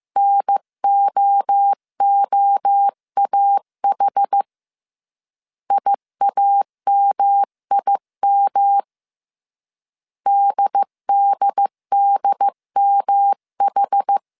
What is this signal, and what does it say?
Morse code, heard as a keyed tone, about 15 wpm, 800 Hz. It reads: NOOAH IAMIM DDDMH
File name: CW / DET / 1 3 1500C S C 14wpm 2458